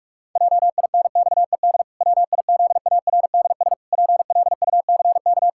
Morse code, heard as a keyed tone, about 47 wpm, 700 Hz. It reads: JINXED WIZARDS PLUCK
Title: Pangram Challenge v2 - 45wpm